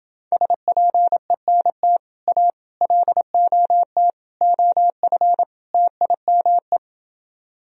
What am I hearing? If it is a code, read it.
Morse code, heard as a keyed tone, about 29 wpm, 700 Hz. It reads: SPENT A LOT OF TIME